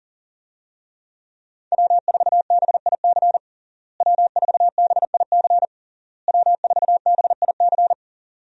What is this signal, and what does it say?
Morse code, heard as a keyed tone, about 40 wpm, 700 Hz. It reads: W4BIC W4BIC W4BIC
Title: USA Call Signs - 5 Characters - Repeated 3x - 40wpm